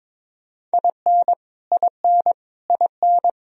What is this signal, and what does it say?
Morse code, heard as a keyed tone, about 22 wpm, 700 Hz. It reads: IN IN IN
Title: Top 100 Words Repeated 22wpm